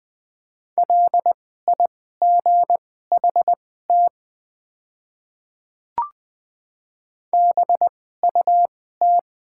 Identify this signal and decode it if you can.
Morse code, heard as a keyed tone, about 20 wpm, spaced as for 14 wpm, 700 Hz. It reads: LIGHT BUT
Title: 2x Character Spacing - Top 300 Words - 20wpm